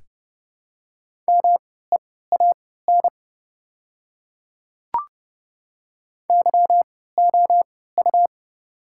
Morse code, heard as a keyed tone, about 29 wpm, spaced as for 17 wpm, 690 Hz. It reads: MEAN YOU
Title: 3x Character Spacing - Top 200 Words - 30wpm